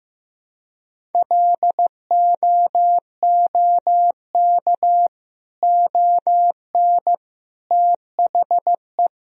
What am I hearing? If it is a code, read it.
Morse code, heard as a keyed tone, about 16 wpm, 700 Hz. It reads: LOOK ON THE
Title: Sentences from Top 1000 Words - Review Part 1 - 15wpm